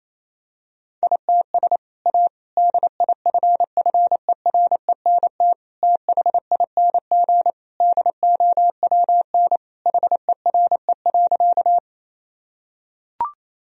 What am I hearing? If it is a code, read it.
Morse code, heard as a keyed tone, about 28 wpm, 700 Hz. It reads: ITS A DIFFERENT THING DOWN HERE.
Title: Sentences from Top 1000 Words - Review Part 3 - 28wpm